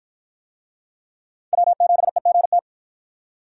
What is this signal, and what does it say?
Morse code, heard as a keyed tone, about 53 wpm, 700 Hz. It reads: W7EGT